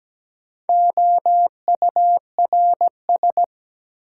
Morse code, heard as a keyed tone, about 17 wpm, 700 Hz. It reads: OURS
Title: Sentences from Top 1300 Words - New Word Drill Alt - 17wpm